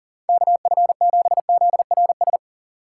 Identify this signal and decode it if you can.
Morse code, heard as a keyed tone, about 40 wpm, 700 Hz. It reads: KF7ZRS
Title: USA Call Signs - 6 Characters - 6x Character Spacing - 40wpm